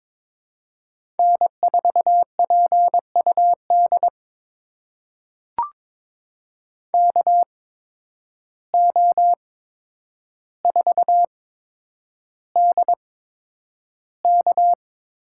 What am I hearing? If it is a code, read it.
Morse code, heard as a keyed tone, about 22 wpm, spaced as for 9 wpm, 700 Hz. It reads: N4PUD KO4DK
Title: USA Call Signs - 5 Characters - 8x Character Spacing - 22wpm